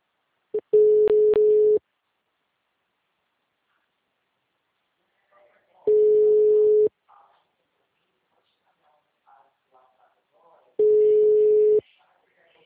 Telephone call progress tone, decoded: ringback tone

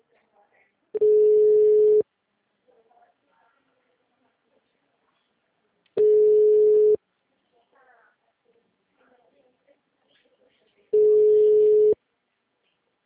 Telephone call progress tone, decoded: ringback tone

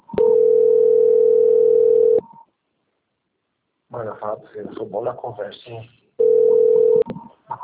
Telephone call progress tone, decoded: ringback tone

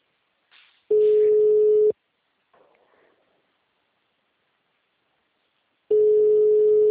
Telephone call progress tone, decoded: ringback tone

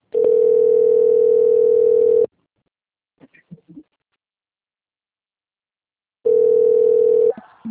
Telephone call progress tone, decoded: ringback tone

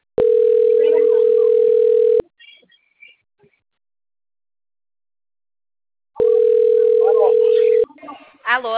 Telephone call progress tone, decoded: ringback tone

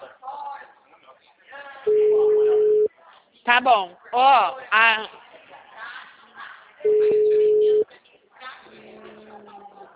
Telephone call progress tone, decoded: ringback tone

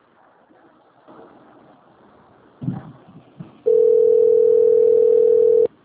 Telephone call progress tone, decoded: ringback tone

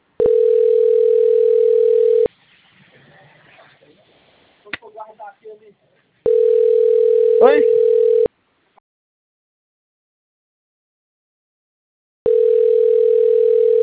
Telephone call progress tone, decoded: ringback tone

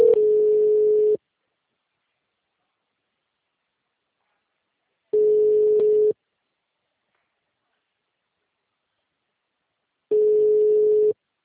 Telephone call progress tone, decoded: ringback tone

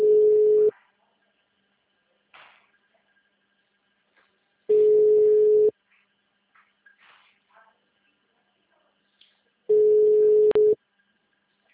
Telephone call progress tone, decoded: ringback tone

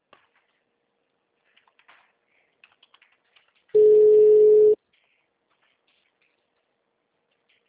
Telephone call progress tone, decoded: ringback tone

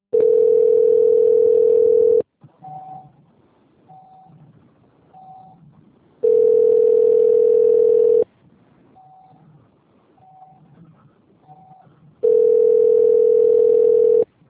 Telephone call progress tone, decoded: ringback tone

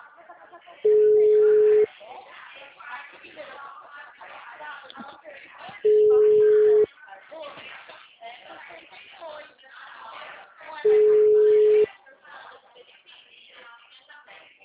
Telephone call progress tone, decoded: ringback tone

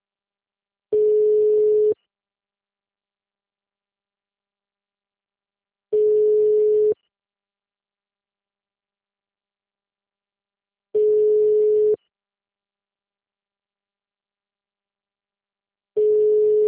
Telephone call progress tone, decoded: ringback tone

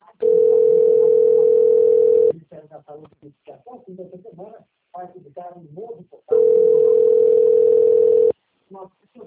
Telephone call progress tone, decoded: ringback tone

